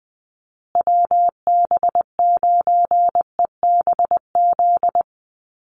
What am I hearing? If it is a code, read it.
Morse code, heard as a keyed tone, about 20 wpm, 700 Hz. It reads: WB9EBZ
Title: USA Call Signs Repeated 3x 20wpm